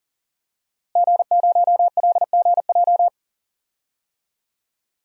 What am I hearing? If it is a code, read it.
Morse code, heard as a keyed tone, about 40 wpm, 700 Hz. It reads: G0LGJ
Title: DXCC-100 Call Signs - 5 Characters - Repeated 2x - 40wpm